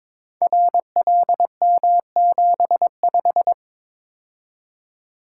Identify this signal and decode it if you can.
Morse code, heard as a keyed tone, about 22 wpm, 700 Hz. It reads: RLM75
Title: Code Groups - 5 Letter:Numbers - 22wpm